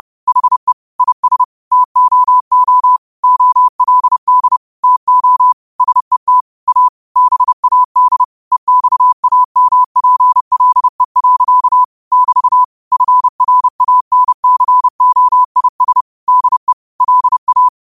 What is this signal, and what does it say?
Morse code, heard as a keyed tone, about 30 wpm, 1000 Hz. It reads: HE IS TOO OLD TO SET A BAD EXAMPLE. <BT> FRANCOIS DE LA
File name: qotdfiles / 30wpm